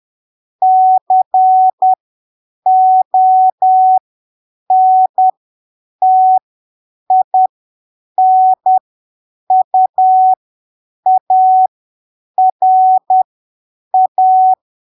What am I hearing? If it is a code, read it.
Morse code, heard as a keyed tone, about 10 wpm, spaced as for 7 wpm, 750 Hz. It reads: CONTINUARA
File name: arquivos / texto / texto 25 10ppm